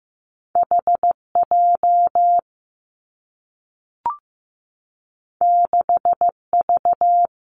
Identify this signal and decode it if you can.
Morse code, heard as a keyed tone, about 15 wpm, 700 Hz. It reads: H J 6 V
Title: Two Challenging Letter-Number Combinations 15wpm